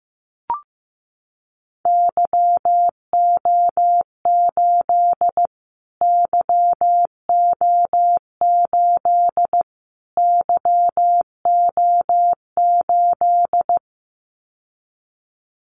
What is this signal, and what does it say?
Morse code, heard as a keyed tone, about 15 wpm, 700 Hz. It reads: YO8 YO8 YO8
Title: Callsign Prefixes for Top 25 DXCC - Repeated 3x - 15wpm